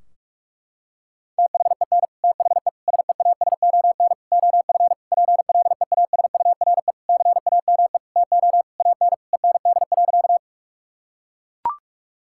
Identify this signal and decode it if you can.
Morse code, heard as a keyed tone, about 41 wpm, 700 Hz. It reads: THEN THE SEASON OF PLEASURE CAME TO AN END.
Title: Sentences from Top 1600 Words 45wpm